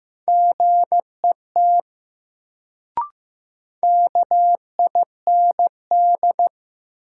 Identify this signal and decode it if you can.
Morse code, heard as a keyed tone, about 15 wpm, 700 Hz. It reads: GET KIND
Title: Sets of 2 Words - New Word Drill - 15wpm